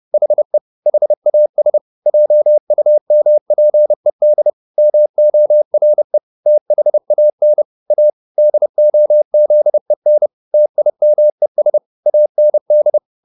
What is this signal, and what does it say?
Morse code, heard as a keyed tone, about 30 wpm, 600 Hz. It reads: HE HAS JUMPED MORE THAN A DOZEN TIMES AND